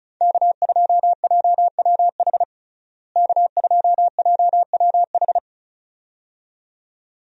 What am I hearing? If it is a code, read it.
Morse code, heard as a keyed tone, about 34 wpm, 700 Hz. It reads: K2JWH K2JWH
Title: USA Call Signs - 5 Characters - Repeated 2x - 35wpm